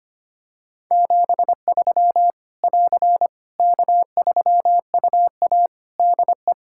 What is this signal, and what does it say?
Morse code, heard as a keyed tone, about 25 wpm, 700 Hz. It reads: 73 <AR> K3UA DE
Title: K1USN SST QSO Practice 25wpm